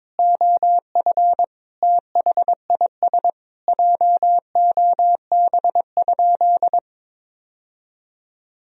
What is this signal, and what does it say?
Morse code, heard as a keyed tone, about 22 wpm, 700 Hz. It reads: OF THIS JOB?
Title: Sentences from Top 1000 Words - Review Part 3 - 22wpm